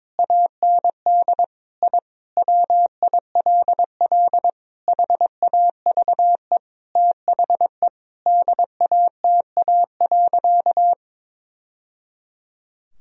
Morse code, heard as a keyed tone, about 22 wpm, 700 Hz. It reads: AND I WILL HAVE THE DATA.
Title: Sentences from Top 1000 Words - Review Part 3 - 22wpm